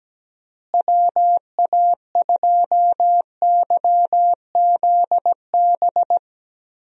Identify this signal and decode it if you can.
Morse code, heard as a keyed tone, about 17 wpm, 700 Hz. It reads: WA2YZB
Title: USA Call Signs - 6 Characters - Repeated 3x - 17wpm